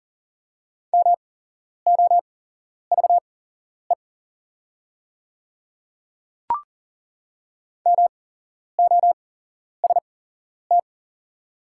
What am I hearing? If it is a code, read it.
Morse code, heard as a keyed tone, about 39 wpm, spaced as for 11 wpm, 690 Hz. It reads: MOVE MOST
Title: 8x Character Spacing - Top 200 Words - 40wpm